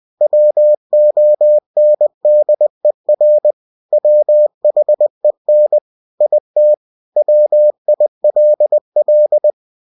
Morse code, heard as a keyed tone, about 20 wpm, 600 Hz. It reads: WONDER WHEN IT WILL
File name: cw-books / winnie-the-pooh / winnie-20wpm-page-0090